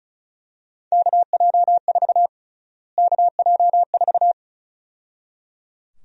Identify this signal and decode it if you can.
Morse code, heard as a keyed tone, about 34 wpm, 700 Hz. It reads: KJ4 KJ4
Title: USA Call Sign Prefixes - Repeated 2x - 35wpm